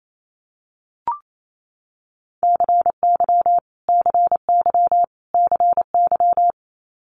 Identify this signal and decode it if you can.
Morse code, heard as a keyed tone, about 28 wpm, 700 Hz. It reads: CY CY CY
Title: Multi-letter Phonemes Repeated 3x 28wpm